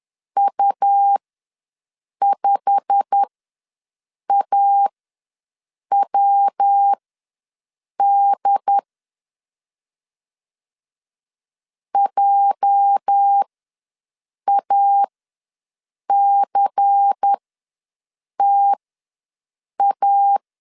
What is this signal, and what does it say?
Morse code, heard as a keyed tone, about 10 wpm, spaced as for 6 wpm, 800 Hz. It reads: U5AWD JACTA
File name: LEZIONI / 8 YJ45 / 8 YJ45 ws 10wpm 3843